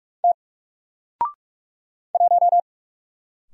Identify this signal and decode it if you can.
Morse code, heard as a keyed tone, about 44 wpm, 700 Hz. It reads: T 1
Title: Single Letter-Number Alt 45wpm